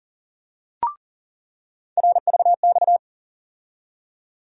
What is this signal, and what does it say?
Morse code, heard as a keyed tone, about 40 wpm, 700 Hz. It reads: RVX
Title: USA Call Sign Postfixes - 3 Letters 40wpm